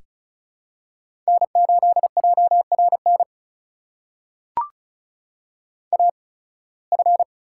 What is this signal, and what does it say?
Morse code, heard as a keyed tone, about 34 wpm, spaced as for 23 wpm, 700 Hz. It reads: N8JRN AF
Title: USA Call Signs - 5 Characters - 8x Character Spacing - 35wpm